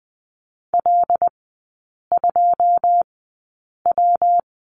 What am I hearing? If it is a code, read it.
Morse code, heard as a keyed tone, about 20 wpm, spaced as for 8 wpm, 690 Hz. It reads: L2W
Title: 133 Lesson 22e - Question Mark - Three Characters 20wpm